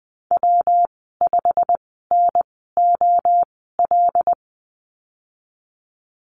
Morse code, heard as a keyed tone, about 19 wpm, spaced as for 14 wpm, 700 Hz. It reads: W5NOL
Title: 173 Lesson 28g - Number 9 - Call Signs 20wpm